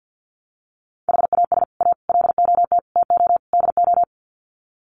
Morse code, heard as a keyed tone, about 50 wpm, 700 Hz. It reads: HAS A LOT TO DO
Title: Sets of 5 Words Encore2 50wpm